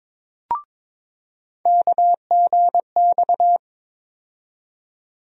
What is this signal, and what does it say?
Morse code, heard as a keyed tone, about 22 wpm, 700 Hz. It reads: KGX
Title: USA Call Sign Postfixes - 3 Letters 22wpm